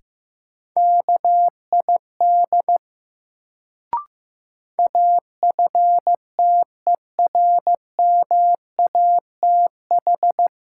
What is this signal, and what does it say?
Morse code, heard as a keyed tone, about 15 wpm, 700 Hz. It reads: KID AFTERMATH
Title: Sets of 5 Words - New Word Drill - 15wpm